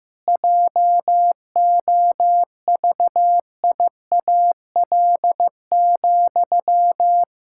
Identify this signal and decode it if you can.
Morse code, heard as a keyed tone, about 16 wpm, 700 Hz. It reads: JOVIAL,
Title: Pangram Challenge v2 - 15wpm